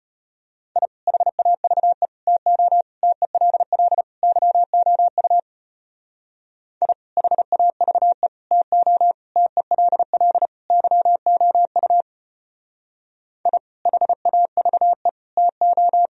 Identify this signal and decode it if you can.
Morse code, heard as a keyed tone, about 32 wpm, 700 Hz. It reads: I HAVE TO TELL YOU I HAVE TO TELL YOU I HAVE TO